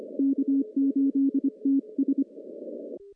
Morse code, heard as a keyed tone, about 25 wpm, 270 Hz. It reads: K8TS
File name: CW-SS-k5tr